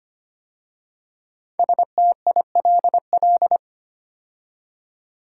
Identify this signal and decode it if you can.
Morse code, heard as a keyed tone, about 25 wpm, 700 Hz. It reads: STILL
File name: 5 Letter Words - First 3 Letters Given - 25wpm